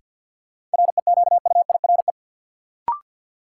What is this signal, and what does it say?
Morse code, heard as a keyed tone, about 53 wpm, 700 Hz. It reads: REQUIRE
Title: Sentences from Top 900 Words - New Word Drill - 50wpm